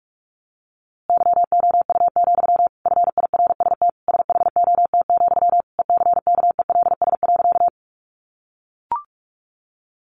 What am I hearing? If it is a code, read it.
Morse code, heard as a keyed tone, about 44 wpm, 700 Hz. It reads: YOU, FIRST SHOT, ECKELS.
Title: A Sound Of Thunder 45wpm